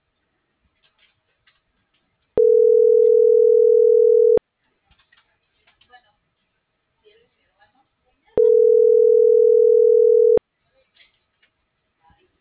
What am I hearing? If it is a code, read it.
Ringback tone (North America), 440 + 480 Hz.
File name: CENTRAL MULTIMODAL / out rodolfo.fuentes